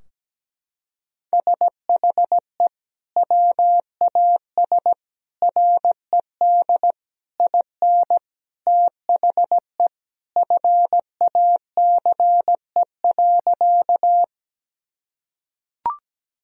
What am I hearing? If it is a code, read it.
Morse code, heard as a keyed tone, about 17 wpm, 700 Hz. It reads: SHE WAS RED IN THE FACE.
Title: Sentences from Top 1000 Words - Review Part 2 - 17wpm